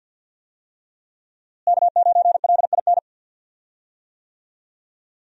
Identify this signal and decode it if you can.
Morse code, heard as a keyed tone, about 51 wpm, 700 Hz. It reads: K9LIN
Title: USA Call Signs - 5 Characters - Repeated 3x - 50wpm